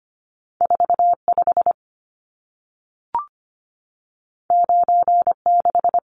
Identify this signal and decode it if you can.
Morse code, heard as a keyed tone, about 25 wpm, 700 Hz. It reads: 45 96
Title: Numbers - 2-Digit 25wpm